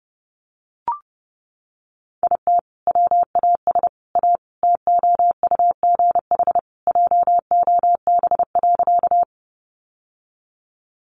Morse code, heard as a keyed tone, about 30 wpm, 700 Hz. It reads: IT WAS A TOUGH JOB.
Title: Sentences from Top 1200 Words 30wpm